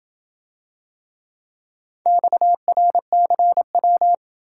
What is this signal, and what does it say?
Morse code, heard as a keyed tone, about 27 wpm, 700 Hz. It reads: XRCW